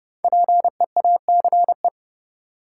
Morse code, heard as a keyed tone, about 30 wpm, 700 Hz. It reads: PEACE
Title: Five Letter Words - 30wpm